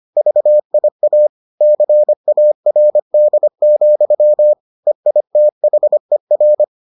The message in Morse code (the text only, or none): VIA CARD, EITHER